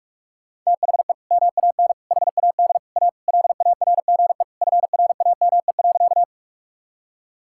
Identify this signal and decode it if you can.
Morse code, heard as a keyed tone, about 46 wpm, 700 Hz. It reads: THE MAN HAD A LARGE FRAME.
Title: Sentences from Top 1000 Words - Review Part 6 - 45wpm